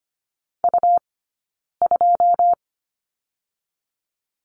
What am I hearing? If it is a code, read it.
Morse code, heard as a keyed tone, about 25 wpm, 700 Hz. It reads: U 2
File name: Lesson 17d - Letter u - Two Characters 25wpm